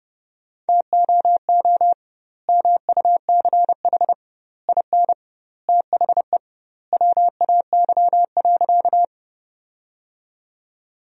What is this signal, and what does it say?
Morse code, heard as a keyed tone, about 32 wpm, 700 Hz. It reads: TOO MUCH IN THE WAY.
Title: 2x Word-Spacing - Sentences from Top 200 Words - 30wpm